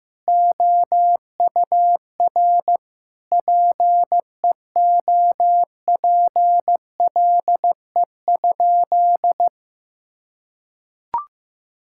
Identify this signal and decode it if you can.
Morse code, heard as a keyed tone, about 16 wpm, 700 Hz. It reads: OUR PEOPLE?
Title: Sentences from Top 1000 Words - Review Part 6 - 15wpm